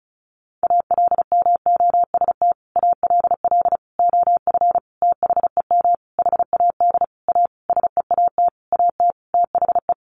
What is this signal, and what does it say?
Morse code, heard as a keyed tone, about 34 wpm, 700 Hz. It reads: ALMOST ALL OF THEM HAD A SEAT AT THE
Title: Sentences from Top 1500 Words 35wpm